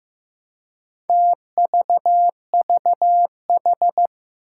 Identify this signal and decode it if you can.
Morse code, heard as a keyed tone, about 15 wpm, 700 Hz. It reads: TVVH
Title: Four Challenging Letter-Number Combinations 15wpm